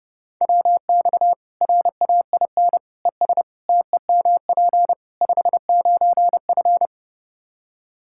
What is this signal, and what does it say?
Morse code, heard as a keyed tone, about 32 wpm, 700 Hz. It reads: WX RAIN ES TEMP 59F
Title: Practice QSO - Exchange 2 - Info - 30wpm